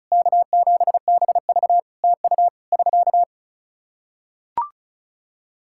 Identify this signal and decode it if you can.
Morse code, heard as a keyed tone, about 36 wpm, 700 Hz. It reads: K7BV TU <SK>
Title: K1USN SST QSO Practice 35wpm